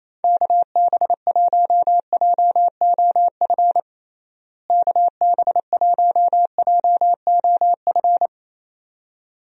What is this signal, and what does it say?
Morse code, heard as a keyed tone, about 28 wpm, 700 Hz. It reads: KB1JOF KB1JOF
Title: USA Call Signs - 6 Characters - Repeated 2x - 28wpm